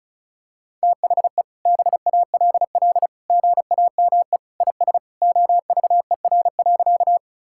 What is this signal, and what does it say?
Morse code, heard as a keyed tone, about 36 wpm, 700 Hz. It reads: THE BALL GAME IS OVER.
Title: Sentences from Top 1000 Words - Review Part 6 - 35wpm